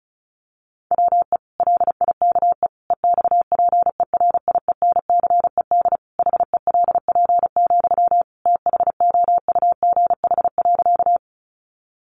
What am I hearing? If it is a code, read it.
Morse code, heard as a keyed tone, about 34 wpm, 700 Hz. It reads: WE LIKE EXPERIENCED HELP, THOUGH.
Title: Sentences from Top 1100 Words 35wpm